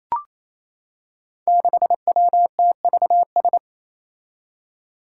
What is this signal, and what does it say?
Morse code, heard as a keyed tone, about 27 wpm, 700 Hz. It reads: 6WTVS
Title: Five Challenging Letter-Number Combinations 28wpm